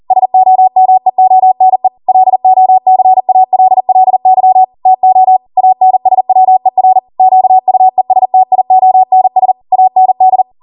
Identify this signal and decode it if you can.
Morse code, heard as a keyed tone, about 40 wpm, 750 Hz. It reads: SOMEONE LOCALLY TO ANSWER QUESTIONS AND